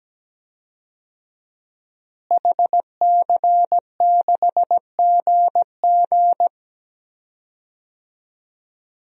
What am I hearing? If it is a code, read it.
Morse code, heard as a keyed tone, about 17 wpm, 700 Hz. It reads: HC6GG
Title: DXCC-100 Call Signs - 5 Characters - Repeated 2x - 17wpm